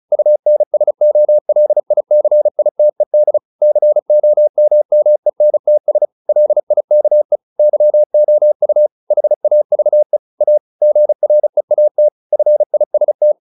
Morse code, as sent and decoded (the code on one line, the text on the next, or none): ..- -. ... --- .-.. .. -.-. .. - . -.. / -.-. --- -- -- . -. - ... / .-.. .. -.- . / -.-- --- ..- / .... .- ...- . / .- / --. .-. .- - / ..-. .. ... -
UNSOLICITED COMMENTS LIKE YOU HAVE A GREAT FIST